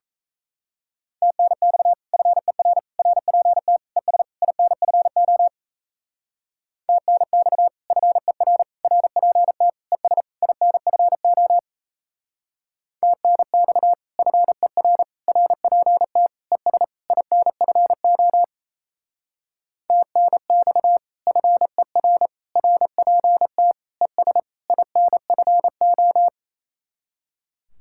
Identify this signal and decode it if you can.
Morse code, heard as a keyed tone, about 36 wpm, 700 Hz. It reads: TNX FER RPT ES INFO TNX FER RPT ES INFO TNX FER RPT ES INFO TNX FER RPT ES INFO